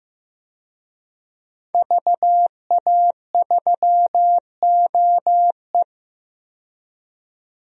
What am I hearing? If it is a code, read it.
Morse code, heard as a keyed tone, about 15 wpm, 700 Hz. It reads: VA3OE
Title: DXCC-100 Call Signs - 5 Characters - 15wpm